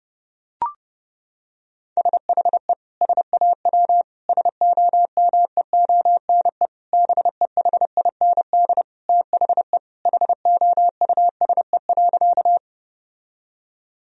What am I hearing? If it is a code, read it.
Morse code, heard as a keyed tone, about 32 wpm, 700 Hz. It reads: SHE SAW SOMEONE BEHIND THE HOUSE.
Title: Sentences from Top 1000 Words - Review Part 4 - 30wpm